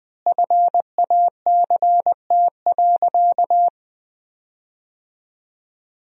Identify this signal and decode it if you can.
Morse code, heard as a keyed tone, about 20 wpm, 700 Hz. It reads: FACT.
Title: 3x Word-Spacing - Sentences from Top 500 Words - 20wpm